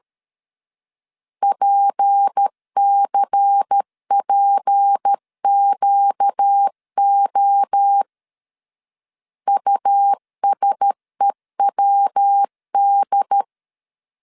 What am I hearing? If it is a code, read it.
Morse code, heard as a keyed tone, about 13 wpm, 800 Hz. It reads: PCPQO USEWD
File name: LEZIONI / 6 PKX0 / 6 P 12wpm 2033